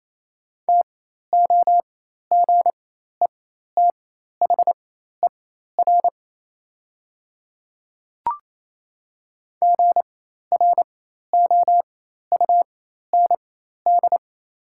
Morse code, heard as a keyed tone, about 28 wpm, spaced as for 13 wpm, 690 Hz. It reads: TOGETHER GROUND
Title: 4x Character Spacing - Top 500 Words - 28wpm